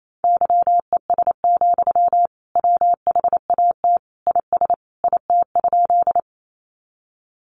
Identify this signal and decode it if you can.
Morse code, heard as a keyed tone, about 28 wpm, 700 Hz. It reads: YES, WHAT IS IT?
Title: Sentences from Top 500 Words 28wpm